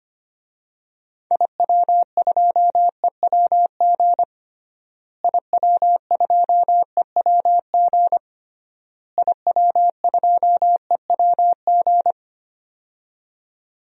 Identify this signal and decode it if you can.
Morse code, heard as a keyed tone, about 25 wpm, 700 Hz. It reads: IW2EWG IW2EWG IW2EWG
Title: DXCC-100 Call Signs - 6 Characters - Repeated 3x - 25wpm